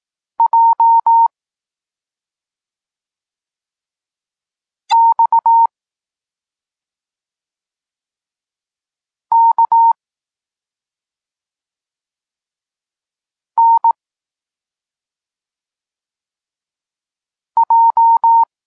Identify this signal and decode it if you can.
Morse code, heard as a keyed tone, about 18 wpm, 920 Hz. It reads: J X K N J
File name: Tekst-34 4